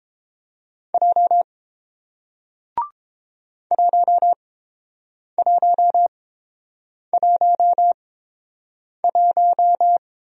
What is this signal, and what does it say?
Morse code, heard as a keyed tone, about 31 wpm, spaced as for 7 wpm, 700 Hz. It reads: J 1111